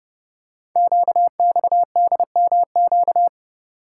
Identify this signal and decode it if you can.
Morse code, heard as a keyed tone, about 30 wpm, 700 Hz. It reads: QXDMQ
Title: Code Groups - 5 Letters - 30wpm